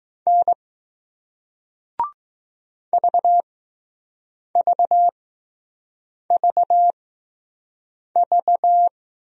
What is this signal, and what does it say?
Morse code, heard as a keyed tone, about 21 wpm, spaced as for 6 wpm, 700 Hz. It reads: N VVVV